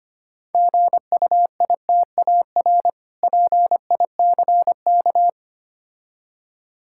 Morse code, heard as a keyed tone, about 26 wpm, 700 Hz. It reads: GUITAR PICK
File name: Easy Words - Speed-Racing with Hints - 25wpm